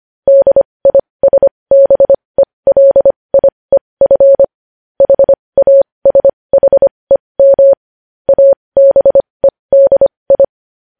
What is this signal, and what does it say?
Morse code, heard as a keyed tone, about 25 wpm, spaced as for 20 wpm, 550 Hz. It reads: DISBELIEF HASHEM ABEDI
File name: podcast / News headlines-1x-25x20